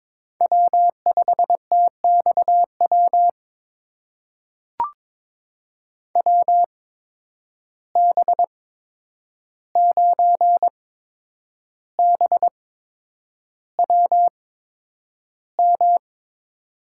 Morse code, heard as a keyed tone, about 22 wpm, spaced as for 9 wpm, 700 Hz. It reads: W5TXW WB9BWM
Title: USA Call Signs - 8x Character Spacing - 22wpm